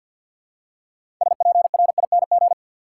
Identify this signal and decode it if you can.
Morse code, heard as a keyed tone, about 53 wpm, 700 Hz. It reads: SPRING
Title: Six Letter Words - 50wpm